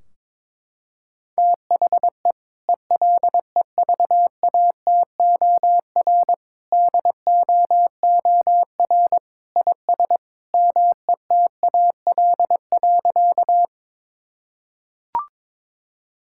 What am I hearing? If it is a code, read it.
Morse code, heard as a keyed tone, about 22 wpm, 700 Hz. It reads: THE ELEVATOR DOOR IS METAL.
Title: Sentences from Top 1600 Words 22wpm